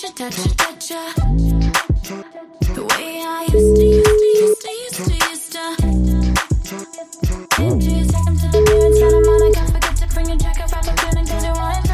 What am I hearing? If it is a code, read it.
Ringback tone (Europe, 425 Hz), with noise.